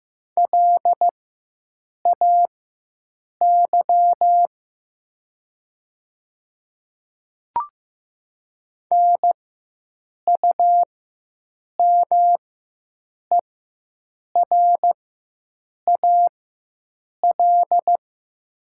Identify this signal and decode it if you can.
Morse code, heard as a keyed tone, about 15 wpm, spaced as for 7 wpm, 700 Hz. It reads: LAY NUMERAL
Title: 4x Character Spacing - Top 500 Words - 15wpm